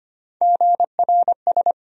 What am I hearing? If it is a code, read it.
Morse code, heard as a keyed tone, about 25 wpm, 700 Hz. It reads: GRS
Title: USA Call Sign Postfixes - 3 Letters 25wpm